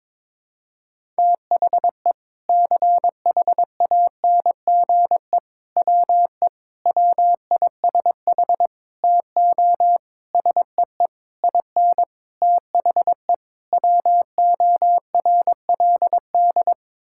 Morse code, heard as a keyed tone, about 22 wpm, 700 Hz. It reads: THE CHANGE WE WISH TO SEE IN THE WORLD
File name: Sets of 9 Words - 22wpm